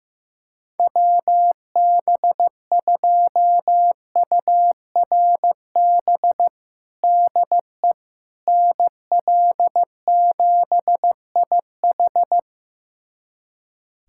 Morse code, heard as a keyed tone, about 16 wpm, 700 Hz. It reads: WB2URB DE NL7IH
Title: USA Callsign DE Callsign 15wpm